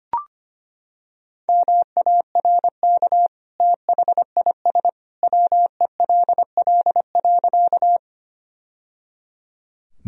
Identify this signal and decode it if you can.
Morse code, heard as a keyed tone, about 25 wpm, 700 Hz. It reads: MARK THIS WELL.
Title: Sentences from Top 1000 Words - Review Part 1 - 25wpm